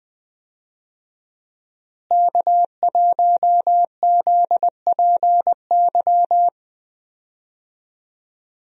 Morse code, heard as a keyed tone, about 20 wpm, 700 Hz. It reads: K1ZPY
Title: USA Call Signs - 5 Characters - Repeated 3x - 20wpm